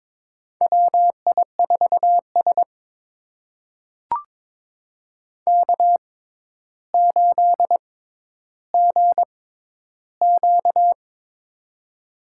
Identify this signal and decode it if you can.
Morse code, heard as a keyed tone, about 22 wpm, spaced as for 11 wpm, 700 Hz. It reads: WI4S K8GQ
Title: USA Call Signs - 4 Characters - 6x Character Spacing - 22wpm